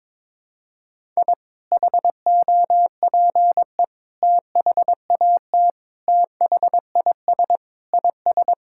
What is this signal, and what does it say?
Morse code, heard as a keyed tone, about 22 wpm, 700 Hz. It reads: I HOPE THAT THIS IS
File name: practice / 22wpm / Sentences with 3-Word Phrases - 22wpm